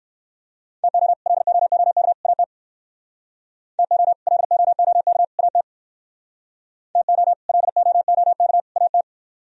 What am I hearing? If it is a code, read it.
Morse code, heard as a keyed tone, about 63 wpm, 700 Hz. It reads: TO LOOK AT TO LOOK AT TO LOOK AT